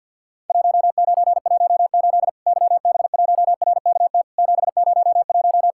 Morse code, heard as a keyed tone, about 52 wpm, 700 Hz. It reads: 1918 YB1RKT 701
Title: Worked All Europe DX Contest - QTC Exchange with Repeats - 50wpm